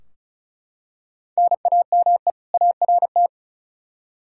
Morse code, heard as a keyed tone, about 36 wpm, 700 Hz. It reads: NAME ART
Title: K1USN SST QSO Practice 35wpm